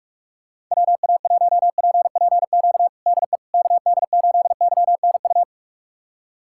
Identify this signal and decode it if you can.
Morse code, heard as a keyed tone, about 46 wpm, 700 Hz. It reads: WA1PPQ DE KD8YNU
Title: USA Callsign DE Callsign - 6 Characters - 45wpm